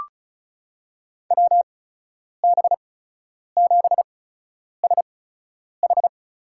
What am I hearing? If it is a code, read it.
Morse code, heard as a keyed tone, about 35 wpm, spaced as for 10 wpm, 700 Hz. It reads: WB7SH
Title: USA Call Signs - 8x Character Spacing - 35wpm